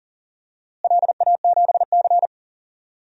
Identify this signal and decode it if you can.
Morse code, heard as a keyed tone, about 40 wpm, 700 Hz. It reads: LA7C
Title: DXCC-100 Call Signs - 4 Characters - Repeated 2x - 40wpm